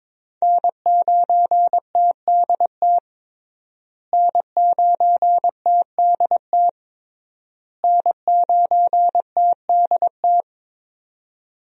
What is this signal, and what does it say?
Morse code, heard as a keyed tone, about 22 wpm, 700 Hz. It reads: N9TDT N9TDT N9TDT
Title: USA Call Signs - 5 Characters - Repeated 3x - 22wpm